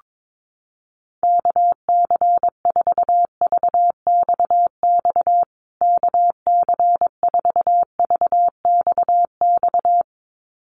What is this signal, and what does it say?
Morse code, heard as a keyed tone, about 22 wpm, 700 Hz. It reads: KC4VXX KC4VXX
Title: USA Call Signs Repeated 2x 22wpm